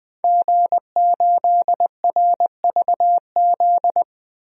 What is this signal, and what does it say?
Morse code, heard as a keyed tone, about 20 wpm, 700 Hz. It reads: G8RVZ
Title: DXCC-100 Call Signs - 5 Characters - 20wpm